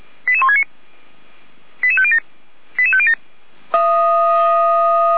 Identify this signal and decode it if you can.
Five-tone selective call (ZVEI), heard as quick runs of five tones, tones of about 70 ms. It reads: 88169 88507 88507
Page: SDR/ZVEI